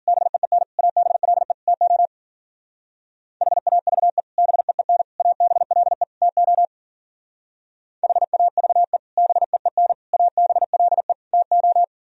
Morse code, heard as a keyed tone, about 52 wpm, 700 Hz. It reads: BEEN ABLE TO HAVE BEEN ABLE TO HAVE BEEN ABLE TO